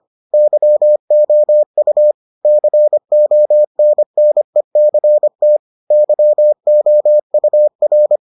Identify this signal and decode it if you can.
Morse code, heard as a keyed tone, about 25 wpm, 600 Hz. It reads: YOU CONNECT YOUR